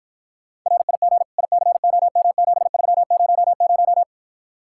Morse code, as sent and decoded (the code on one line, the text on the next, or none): .-. .. --. / .. -.-. --- -- --... ...-- ----- -----
RIG ICOM7300